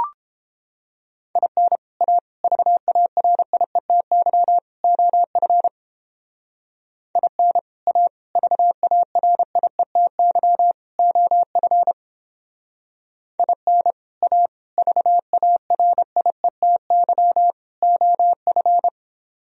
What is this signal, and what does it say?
Morse code, heard as a keyed tone, about 29 wpm, 700 Hz. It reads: IN A VARIETY OF IN A VARIETY OF IN A VARIETY OF